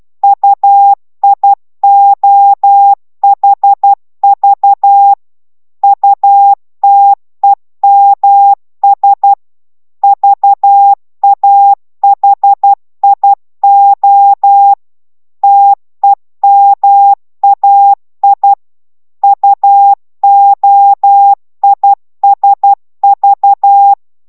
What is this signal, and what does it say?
Morse code, heard as a keyed tone, about 12 wpm, 800 Hz. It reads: UIOHV UTEMS VAHIO TEMAI UOISV
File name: taviro tanulas / cw 69 05 10